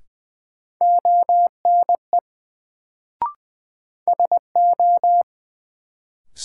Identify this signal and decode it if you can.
Morse code, heard as a keyed tone, about 21 wpm, 700 Hz. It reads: ONE SO
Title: Sets of 2 Words - New Word Drill - 20wpm